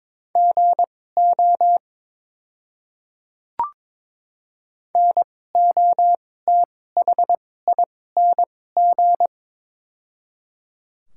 Morse code, heard as a keyed tone, about 22 wpm, spaced as for 16 wpm, 700 Hz. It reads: GO NOTHING
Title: 2x Character Spacing - Top 500 Words - 22wpm